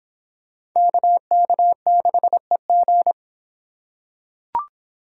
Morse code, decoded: KK6EG

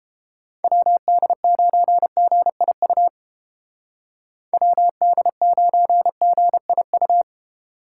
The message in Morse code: WD9GIU WD9GIU

